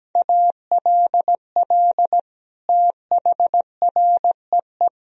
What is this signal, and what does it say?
Morse code, heard as a keyed tone, about 17 wpm, 700 Hz. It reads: ALL THREE
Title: Sentences from Top 1000 Words - Review Part 4 - 17wpm